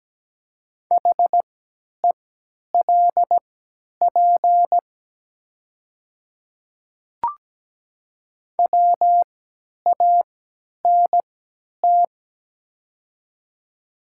Morse code, heard as a keyed tone, about 17 wpm, spaced as for 10 wpm, 700 Hz. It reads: HELP WANT
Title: 3x Character Spacing - Top 300 Words - 17wpm